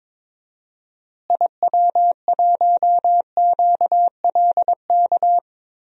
Morse code, heard as a keyed tone, about 22 wpm, 700 Hz. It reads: IW1QLK
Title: DXCC-100 Call Signs - 6 Characters - Repeated 2x - 22wpm